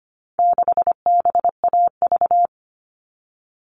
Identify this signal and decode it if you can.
Morse code, heard as a keyed tone, about 25 wpm, 700 Hz. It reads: 6BAV
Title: All Cut Numbers - 4-Digit 25wpm